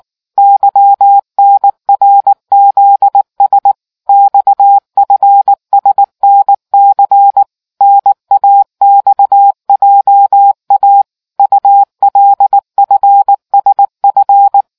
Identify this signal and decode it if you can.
Morse code, heard as a keyed tone, about 19 wpm, 800 Hz. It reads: YNRZS XFSNC NAXJA ULFSF